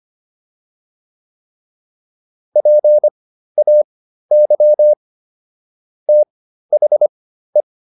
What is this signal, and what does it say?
Morse code, heard as a keyed tone, about 25 wpm, spaced as for 13 wpm, 600 Hz. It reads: PAY THE
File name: PR202 13